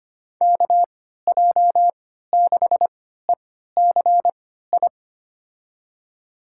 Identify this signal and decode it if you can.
Morse code, heard as a keyed tone, about 25 wpm, spaced as for 14 wpm, 700 Hz. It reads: KJ6ECI
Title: USA Call Signs - 6 Characters - 3x Character Spacing - 25wpm